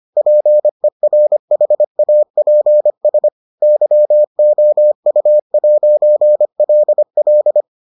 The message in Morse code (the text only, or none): PERHAPS YOU'LL